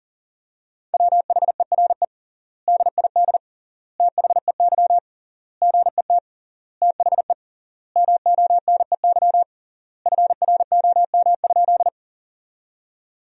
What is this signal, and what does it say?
Morse code, heard as a keyed tone, about 40 wpm, 700 Hz. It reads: WHERE DID THEY GET THE MONEY FROM?
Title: 3x Word-Spacing - Sentences from Top 400 Words - 40wpm